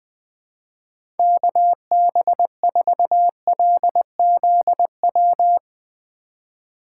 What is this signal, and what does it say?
Morse code, heard as a keyed tone, about 20 wpm, 700 Hz. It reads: KB4LZW
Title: USA Call Signs - 6 Characters - 3x Character Spacing - 20wpm